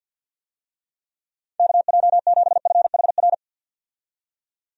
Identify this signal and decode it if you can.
Morse code, heard as a keyed tone, about 52 wpm, 700 Hz. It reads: KJ7FHR
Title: USA Call Signs - 6 Characters - 50wpm